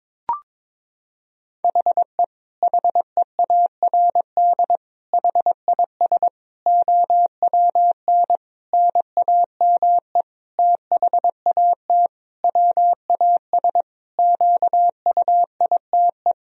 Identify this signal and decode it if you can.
Morse code, heard as a keyed tone, about 22 wpm, 700 Hz. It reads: HE HEARD HIS OWN NAME THAT WAS QUITE
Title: Sentences from Top 1000 Words - Review Part 2 - 22wpm